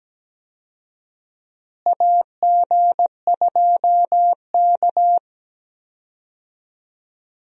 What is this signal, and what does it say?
Morse code, heard as a keyed tone, about 17 wpm, 700 Hz. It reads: AG2K
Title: USA Call Signs - 4 Characters - Repeated 3x - 17wpm